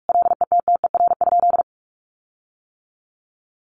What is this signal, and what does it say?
Morse code, heard as a keyed tone, about 44 wpm, 700 Hz. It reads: LETTER?
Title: Sentences from Top 1300 Words 45wpm